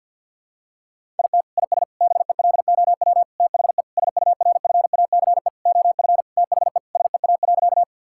Morse code, heard as a keyed tone, about 50 wpm, 700 Hz. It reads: IT IS BELOW THE SURFACE OF THE SEA.
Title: Sentences from Top 1000 Words - Review Part 5 - 50wpm